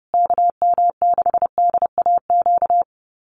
Morse code, heard as a keyed tone, about 30 wpm, 700 Hz. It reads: KM6DAQ
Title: USA Call Signs Repeated 3x 30wpm